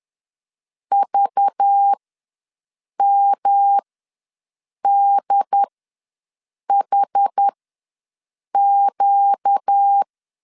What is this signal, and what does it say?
Morse code, heard as a keyed tone, about 10 wpm, spaced as for 6 wpm, 800 Hz. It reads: VMDHQ
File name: LEZIONI / 5 LBVC / 5 LBV ws 10wpm 3645